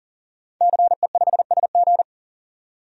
Morse code, heard as a keyed tone, about 40 wpm, 700 Hz. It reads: CE5SG
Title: DXCC-100 Call Signs - 5 Characters - 40wpm